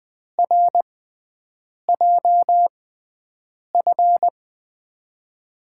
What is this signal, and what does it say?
Morse code, heard as a keyed tone, about 20 wpm, spaced as for 7 wpm, 700 Hz. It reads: RJF
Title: USA Call Signs - 6x Character Spacing - 20wpm